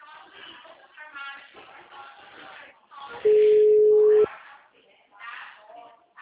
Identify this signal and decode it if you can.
Ringback tone (Europe, 425 Hz), with speech.
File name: out- -2010- - - .85003